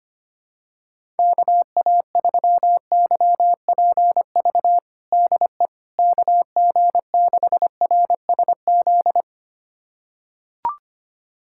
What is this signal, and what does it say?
Morse code, heard as a keyed tone, about 25 wpm, 700 Hz. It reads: KA3YPV DE KG6RSZ
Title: USA Callsign DE Callsign 25wpm